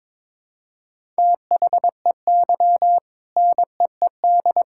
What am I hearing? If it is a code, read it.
Morse code, heard as a keyed tone, about 22 wpm, 700 Hz. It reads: THEY NEED
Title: Sentences from Top 1000 Words - Review Part 6 - 22wpm